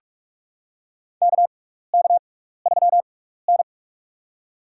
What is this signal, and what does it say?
Morse code, heard as a keyed tone, about 42 wpm, spaced as for 15 wpm, 700 Hz. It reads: KK3N
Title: USA Call Signs - 4 Characters - 6x Character Spacing - 45wpm